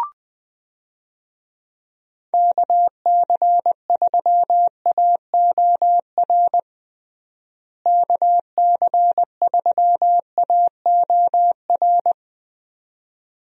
Morse code, heard as a keyed tone, about 20 wpm, 700 Hz. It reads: KC3AOR KC3AOR
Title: USA Call Signs - 6 Characters - Repeated 2x - 20wpm